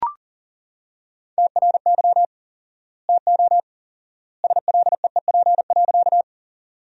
Morse code, heard as a keyed tone, about 40 wpm, 700 Hz. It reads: TRY TO SLEEP.
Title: 4x Word-Spacing - Sentences from Top 500 Words - 40wpm